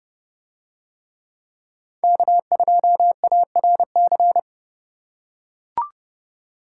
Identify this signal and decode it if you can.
Morse code, heard as a keyed tone, about 30 wpm, 700 Hz. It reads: K2ARC